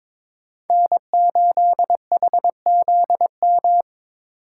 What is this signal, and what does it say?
Morse code, heard as a keyed tone, about 22 wpm, 700 Hz. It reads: N8HZM
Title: K1USN SST QSO Practice - 22wpm